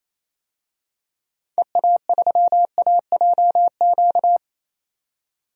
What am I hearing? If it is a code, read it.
Morse code, heard as a keyed tone, about 28 wpm, 700 Hz. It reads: EA3AJQ
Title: DXCC-100 Call Signs - 6 Characters - 28wpm